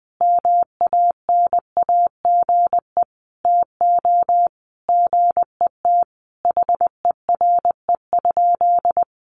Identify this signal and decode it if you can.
Morse code, heard as a keyed tone, about 20 wpm, 700 Hz. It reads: MANAGE TO GET HERE?
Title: Sentences from Top 1400 Words 20wpm